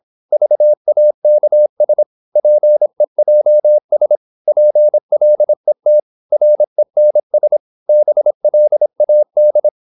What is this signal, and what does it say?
Morse code, heard as a keyed tone, about 26 wpm, 600 Hz. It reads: VAKS PEJS PLET RENS BLAD
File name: Korte ord DK 26wpm